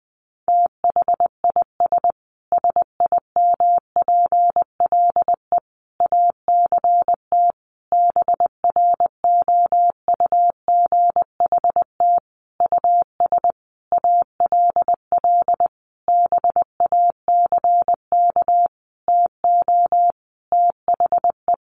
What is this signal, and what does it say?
Morse code, heard as a keyed tone, about 19 wpm, 700 Hz. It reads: THIS SIMPLE ACT BROUGHT US ALL BACK TO THE
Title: Sentences from Top 1500 Words 20wpm